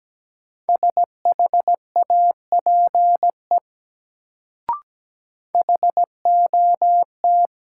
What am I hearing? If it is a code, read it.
Morse code, heard as a keyed tone, about 17 wpm, 700 Hz. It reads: SHAPE HOT
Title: Sentences from Top 500 Words - New Word Drill - 17wpm